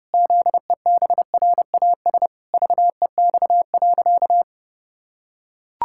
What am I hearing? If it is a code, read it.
Morse code, heard as a keyed tone, about 30 wpm, 700 Hz. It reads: ZEBRAS VEX.
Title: Pangram Challenge v2 - 30wpm